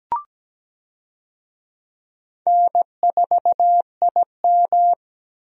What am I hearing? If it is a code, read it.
Morse code, heard as a keyed tone, about 17 wpm, 700 Hz. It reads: N4IM
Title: USA Call Signs - 4 Characters - Repeated 2x - 17wpm